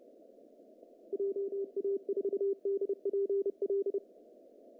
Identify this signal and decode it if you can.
Morse code, heard as a keyed tone, about 28 wpm, 400 Hz. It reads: JA4DPL
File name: CW / cqww-cw-k5tr